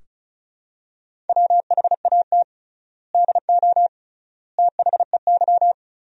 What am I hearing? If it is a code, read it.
Morse code, heard as a keyed tone, about 36 wpm, 700 Hz. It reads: WHAT DO THEY
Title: 3x Word-Spacing - Sentences from Top 200 Words - 35wpm